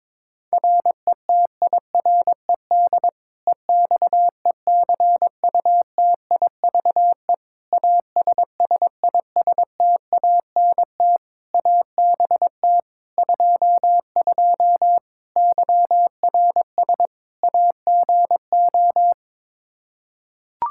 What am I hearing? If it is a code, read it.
Morse code, heard as a keyed tone, about 21 wpm, 700 Hz. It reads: RETIRED EXECUTIVE ASSISTANT ABT 22 YRS AGO
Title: Practice QSO - Exchange 3 - More Info - 22wpm